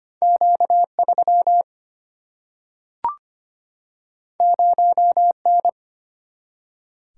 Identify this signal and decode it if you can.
Morse code, heard as a keyed tone, about 25 wpm, 700 Hz. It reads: Q3 0N
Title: Single-Letter Single-Number 25wpm